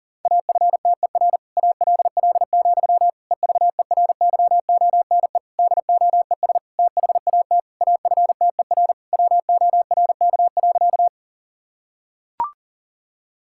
Morse code, heard as a keyed tone, about 40 wpm, 700 Hz. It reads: AFTER ALL, EVERYONE DOES THAT AFTER WORK.
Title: Sentences from Top 1000 Words - Review Part 5 - 40wpm